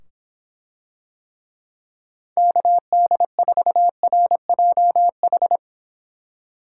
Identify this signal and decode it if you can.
Morse code, heard as a keyed tone, about 24 wpm, 700 Hz. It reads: KD4RJH